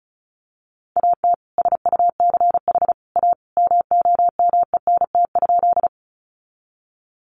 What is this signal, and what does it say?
Morse code, heard as a keyed tone, about 34 wpm, 700 Hz. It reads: AT SUCH A MOMENT?
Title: Sentences from Top 1000 Words 35wpm